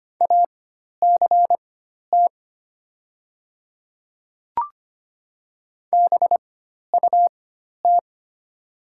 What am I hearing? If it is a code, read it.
Morse code, heard as a keyed tone, about 25 wpm, spaced as for 12 wpm, 690 Hz. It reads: ACT BUT
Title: 4x Character Spacing - Top 200 Words - 25wpm